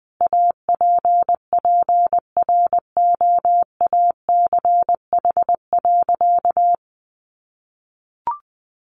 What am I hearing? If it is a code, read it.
Morse code, heard as a keyed tone, about 20 wpm, 700 Hz. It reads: APPROACH.
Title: Sentences from Top 1400 Words 20wpm